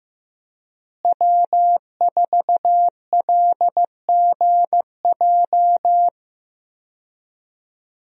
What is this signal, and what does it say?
Morse code, heard as a keyed tone, about 15 wpm, 700 Hz. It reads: W4LGJ
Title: USA Call Signs - 5 Characters - Repeated 2x - 15wpm